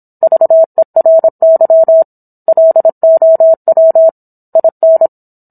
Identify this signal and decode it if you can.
Morse code, heard as a keyed tone, about 27 wpm, 650 Hz. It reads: VERY LOW IN